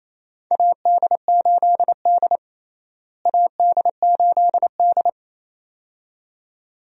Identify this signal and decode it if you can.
Morse code, heard as a keyed tone, about 28 wpm, 700 Hz. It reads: AD8D AD8D
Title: USA Call Signs - 4 Characters - Repeated 2x - 28wpm